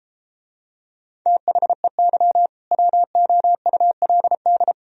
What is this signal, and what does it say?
Morse code, heard as a keyed tone, about 32 wpm, 700 Hz. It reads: THEY WOULD